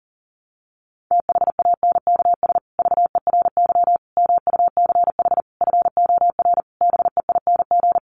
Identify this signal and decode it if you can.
Morse code, heard as a keyed tone, about 40 wpm, 700 Hz. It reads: THANKS VERY MUCH FOR BEING
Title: Sets of 5 Words Encore 40wpm